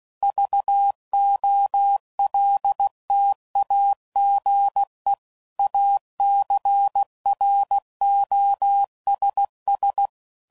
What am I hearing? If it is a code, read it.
Morse code, heard as a keyed tone, about 16 wpm, 800 Hz. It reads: VOLTAGE ACROSS
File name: CW Code Groups @15WPM / Technical Subjects / Reactance